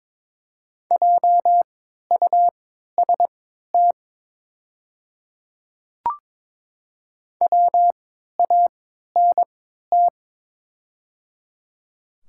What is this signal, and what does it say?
Morse code, heard as a keyed tone, about 22 wpm, spaced as for 13 wpm, 700 Hz. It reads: JUST WANT
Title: 3x Character Spacing - Top 200 Words - 22wpm